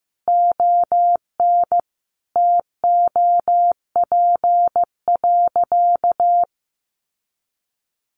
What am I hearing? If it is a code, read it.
Morse code, heard as a keyed tone, about 15 wpm, 700 Hz. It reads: ON TOP.
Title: Sentences from Top 1300 Words 15wpm